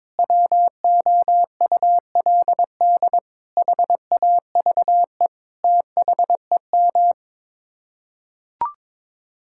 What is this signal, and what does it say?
Morse code, heard as a keyed tone, about 22 wpm, 700 Hz. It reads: WOULD HAVE THEM
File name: practice / 22wpm / Sets of 8 Words - 22wpm